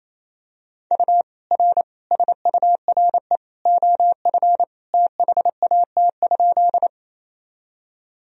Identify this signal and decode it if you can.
Morse code, heard as a keyed tone, about 28 wpm, 700 Hz. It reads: U R SURE OF THAT?
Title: Sentences from Top 300 Words and Abbreviations 28wpm